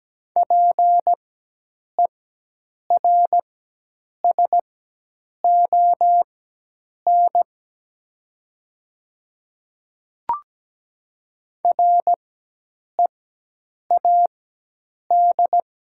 Morse code, heard as a keyed tone, about 17 wpm, spaced as for 8 wpm, 700 Hz. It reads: PERSON READ
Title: 4x Character Spacing - Top 500 Words - 17wpm